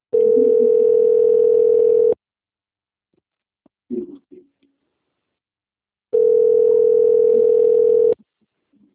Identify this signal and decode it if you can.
Ringback tone (North America), 440 + 480 Hz.
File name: out- - - - - .52112